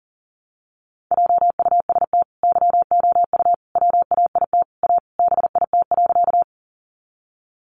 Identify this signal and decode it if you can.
Morse code, heard as a keyed tone, about 40 wpm, 700 Hz. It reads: JUST YOU WAIT A BIT.
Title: Sentences from Top 900 Words 40wpm